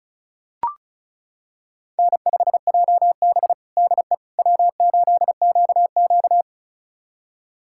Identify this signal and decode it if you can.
Morse code, heard as a keyed tone, about 36 wpm, 700 Hz. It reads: N5JB DE W8QQ
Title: USA Callsign DE Callsign - 4 Characters - 35wpm